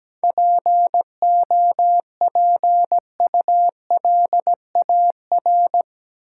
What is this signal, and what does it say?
Morse code, heard as a keyed tone, about 17 wpm, 700 Hz. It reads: POPULAR
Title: Sentences from Top 1000 Words - Review Part 3 - New Word Drill - 17wpm